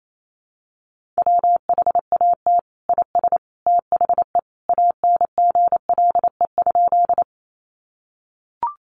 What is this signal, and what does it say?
Morse code, heard as a keyed tone, about 28 wpm, 700 Hz. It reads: WHAT IS THE ANGLE?
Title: Sentences from Top 1500 Words 28wpm